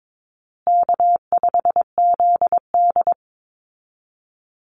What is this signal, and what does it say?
Morse code, heard as a keyed tone, about 22 wpm, 700 Hz. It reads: K5ZD
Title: Easy Call Signs Repeated 3x - 22wpm